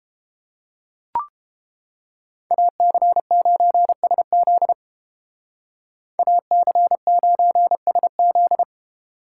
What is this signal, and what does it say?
Morse code, heard as a keyed tone, about 30 wpm, 700 Hz. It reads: AC9SZ AC9SZ